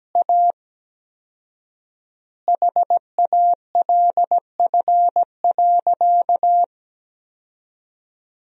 Morse code, heard as a keyed tone, about 17 wpm, 700 Hz. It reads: A HALF.
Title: 4x Word-Spacing - Sentences from Top 400 Words - 17wpm